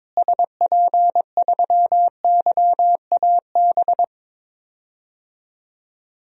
Morse code, heard as a keyed tone, about 22 wpm, 700 Hz. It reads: SP3YAB
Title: DXCC-100 Call Signs - 6 Characters - Repeated 2x - 22wpm